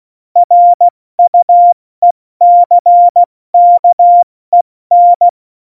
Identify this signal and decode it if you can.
Morse code, heard as a keyed tone, about 16 wpm, spaced as for 14 wpm, 700 Hz. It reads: RUECKEN
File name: Morsekurs / Witze / Witz 17. Oktober 17, Tempo 70